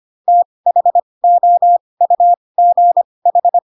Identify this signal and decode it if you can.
Morse code, heard as a keyed tone, about 25 wpm, spaced as for 20 wpm, 700 Hz. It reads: THOUGH
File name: Pinocchio / Pinocchio ch8 15 25 ik1hge